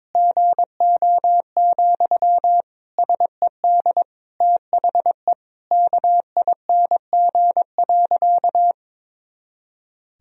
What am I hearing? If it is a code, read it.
Morse code, heard as a keyed tone, about 22 wpm, 700 Hz. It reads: GO, SED THE KING.
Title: Sentences from Top 400 Words and Abbreviations 22wpm